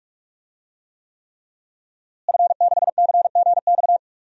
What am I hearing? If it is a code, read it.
Morse code, heard as a keyed tone, about 46 wpm, 700 Hz. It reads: F6CGX
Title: DXCC-100 Call Signs - 5 Characters - Repeated 3x - 45wpm